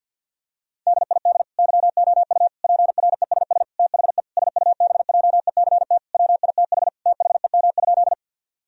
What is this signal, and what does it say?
Morse code, heard as a keyed tone, about 51 wpm, 700 Hz. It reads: DID YOU PRESS THE SUBJECT WITH THEM?
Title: Sentences from Top 1000 Words - Review Part 1 - 50wpm